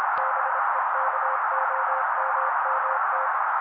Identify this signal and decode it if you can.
Morse code, heard as a keyed tone, about 26 wpm, 530 Hz. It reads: BEKOMMT
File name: daily / 25wpm noise